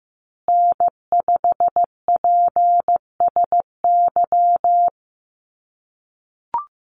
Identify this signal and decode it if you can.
Morse code, heard as a keyed tone, about 15 wpm, 700 Hz. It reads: N5PSY
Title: USA Call Signs Repeated 2x 15wpm